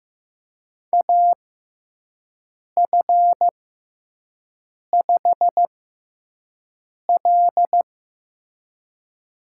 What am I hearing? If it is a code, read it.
Morse code, heard as a keyed tone, about 15 wpm, spaced as for 5 wpm, 700 Hz. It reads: AF5L
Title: USA Call Signs - 6x Character Spacing - 15wpm